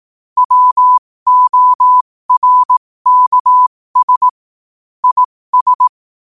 Morse code, heard as a keyed tone, about 18 wpm, spaced as for 15 wpm, 1000 Hz. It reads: WORKS IS
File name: qotdfiles / 15wpm